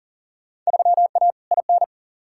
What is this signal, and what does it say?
Morse code, heard as a keyed tone, about 40 wpm, 700 Hz. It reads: 3A IN
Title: Field Day Exchanges - 40wpm